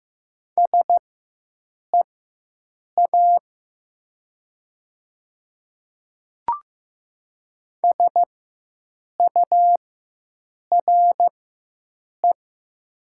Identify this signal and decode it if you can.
Morse code, heard as a keyed tone, about 15 wpm, spaced as for 7 wpm, 700 Hz. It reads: SEA SURE
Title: 4x Character Spacing - Top 300 Words - 15wpm